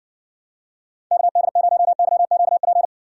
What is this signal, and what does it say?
Morse code, heard as a keyed tone, about 66 wpm, 700 Hz. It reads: KD0YQP